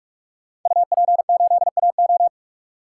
Morse code, heard as a keyed tone, about 46 wpm, 700 Hz. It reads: UP8AO
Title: Code Groups - 5 Letter:Numbers - 45wpm